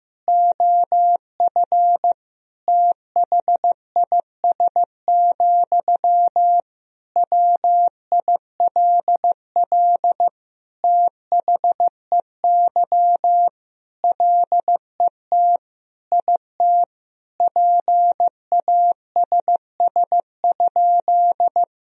Morse code, heard as a keyed tone, about 15 wpm, 700 Hz. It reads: OF THIS, WILL THEY LET IT PASS?
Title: Sentences from Top 400 Words and Abbreviations 15wpm